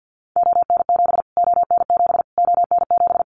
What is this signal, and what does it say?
Morse code, heard as a keyed tone, about 50 wpm, 700 Hz. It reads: ON7 ON7 ON7
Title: Callsign Prefixes for Top 25 DXCC - Repeated 3x - 50wpm